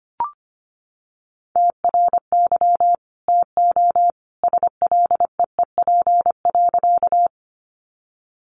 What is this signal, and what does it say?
Morse code, heard as a keyed tone, about 25 wpm, 700 Hz. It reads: TRY TO SLEEP.
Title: Sentences from Top 500 Words 25wpm